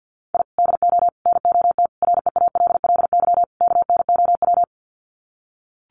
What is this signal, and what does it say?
Morse code, heard as a keyed tone, about 49 wpm, 700 Hz. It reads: I DO NOT REALLY KNOW
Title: Sets of 5 Words Encore 50wpm